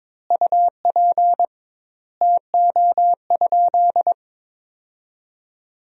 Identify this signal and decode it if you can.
Morse code, heard as a keyed tone, about 21 wpm, 700 Hz. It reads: UP TO?
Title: 2x Word-Spacing - Sentences from Top 100 Words - 22wpm